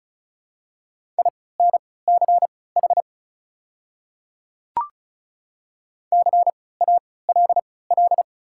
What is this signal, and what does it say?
Morse code, heard as a keyed tone, about 36 wpm, spaced as for 20 wpm, 700 Hz. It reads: INCH CALL
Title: 3x Character Spacing - Top 500 Words - 35wpm